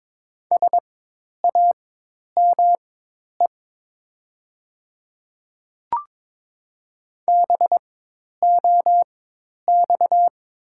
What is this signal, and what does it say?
Morse code, heard as a keyed tone, about 22 wpm, spaced as for 10 wpm, 700 Hz. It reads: SAME BOX
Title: 4x Character Spacing - Top 500 Words - 22wpm